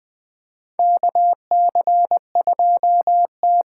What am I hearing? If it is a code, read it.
Morse code, heard as a keyed tone, about 20 wpm, 700 Hz. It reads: KC2T
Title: USA Call Signs - 6 Characters - 8x Character Spacing - 20wpm